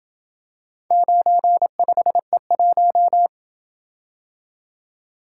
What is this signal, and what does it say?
Morse code, heard as a keyed tone, about 27 wpm, 700 Hz. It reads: 95E1